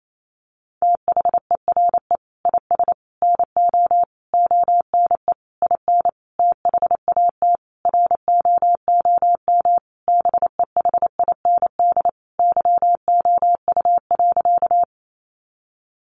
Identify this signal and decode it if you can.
Morse code, heard as a keyed tone, about 28 wpm, 700 Hz. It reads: THERE IS NO ONE IN THAT ROOM BEHIND YOU.
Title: Sentences from Top 500 Words 28wpm